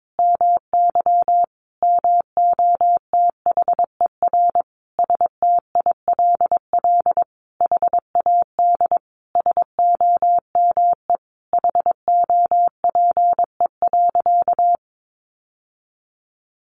Morse code, heard as a keyed tone, about 22 wpm, 700 Hz. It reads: MY MOTHER STILL HAD SOME HOPE.
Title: Sentences from Top 1000 Words 22wpm